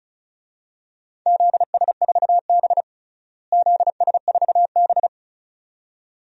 Morse code, heard as a keyed tone, about 36 wpm, 700 Hz. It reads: ZS4B ZS4B
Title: DXCC-100 Call Signs - 4 Characters - Repeated 2x - 35wpm